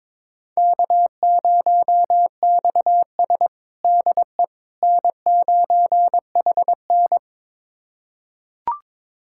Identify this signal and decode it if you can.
Morse code, heard as a keyed tone, about 22 wpm, 700 Hz. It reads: K0XS DE N9HN
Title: USA Callsign DE Callsign - 4 Characters - 22wpm